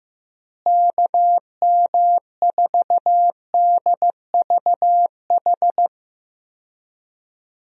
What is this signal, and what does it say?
Morse code, heard as a keyed tone, about 15 wpm, 700 Hz. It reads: KM4DVH
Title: USA Call Signs - 6 Characters - Repeated 3x - 15wpm